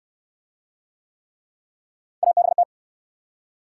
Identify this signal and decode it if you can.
Morse code, heard as a keyed tone, about 70 wpm, 700 Hz. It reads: ABT